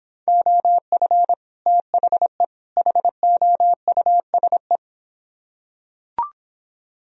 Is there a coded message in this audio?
Yes: Morse code from 0.3 to 4.8 s, heard as a keyed tone, about 24 wpm, 700 Hz. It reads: OF THE HOUSE